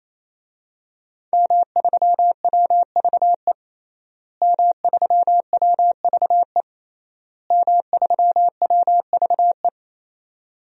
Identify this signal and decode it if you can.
Morse code, heard as a keyed tone, about 27 wpm, 700 Hz. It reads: M3WVE M3WVE M3WVE
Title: DXCC-100 Call Signs - 5 Characters - Repeated 3x - 28wpm